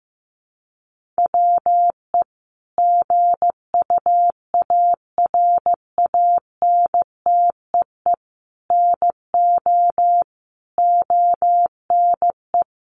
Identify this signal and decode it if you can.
Morse code, heard as a keyed tone, about 15 wpm, 700 Hz. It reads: WE GUARANTEE NO ONE
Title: A Sound Of Thunder 15wpm